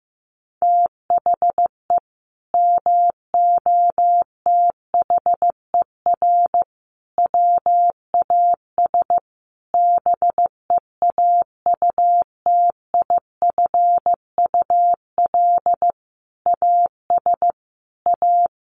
Morse code, heard as a keyed tone, about 15 wpm, 700 Hz. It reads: THE MOTHER WAS BEAUTIFUL AS A
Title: Sentences from Top 1500 Words 15wpm